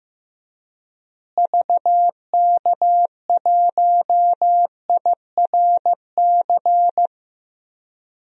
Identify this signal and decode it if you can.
Morse code, heard as a keyed tone, about 15 wpm, 700 Hz. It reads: VK1IRC
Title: DXCC-100 Call Signs - 6 Characters - Repeated 2x - 15wpm